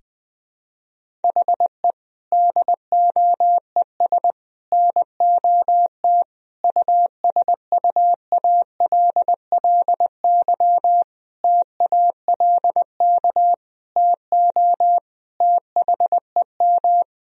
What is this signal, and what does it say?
Morse code, heard as a keyed tone, about 21 wpm, 700 Hz. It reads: HE DOES NOT USUALLY TALK TO THEM
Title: Sentences from Top 1000 Words - Review Part 3 - 20wpm